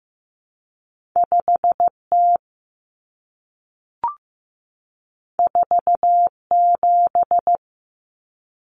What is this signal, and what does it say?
Morse code, heard as a keyed tone, about 15 wpm, 700 Hz. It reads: 5T 47
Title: Common Cut Numbers - 2-Digit 15wpm